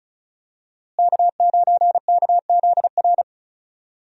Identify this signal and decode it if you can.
Morse code, heard as a keyed tone, about 36 wpm, 700 Hz. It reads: K9KZR
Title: USA Call Signs - 5 Characters - Repeated 2x - 35wpm